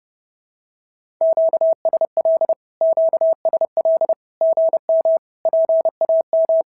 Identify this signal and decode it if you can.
Morse code, heard as a keyed tone, about 30 wpm, 650 Hz. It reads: QSL QSL GM PAM
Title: SOTA QSO Practice 30wpm